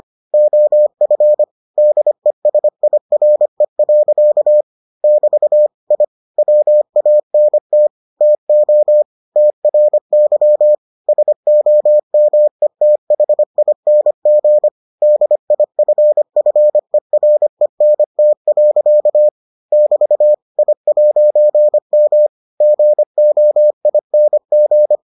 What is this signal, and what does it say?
Morse code, heard as a keyed tone, about 25 wpm, 600 Hz. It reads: OF DESIRE. <BT> I WANT TO TRY SOMETHING DIFFERENT. <BT> I'M GOING